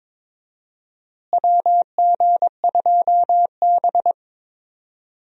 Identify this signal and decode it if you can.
Morse code, heard as a keyed tone, about 22 wpm, 700 Hz. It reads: WG2B
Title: USA Call Signs - 4 Characters - Repeated 2x - 22wpm